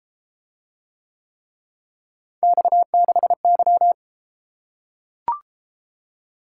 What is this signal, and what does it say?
Morse code, heard as a keyed tone, about 32 wpm, 700 Hz. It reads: X6Y